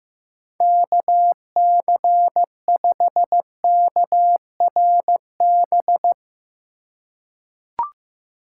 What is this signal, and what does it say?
Morse code, heard as a keyed tone, about 15 wpm, 700 Hz. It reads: KC5KRB
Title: USA Call Signs - 6 Characters - 3x Character Spacing - 15wpm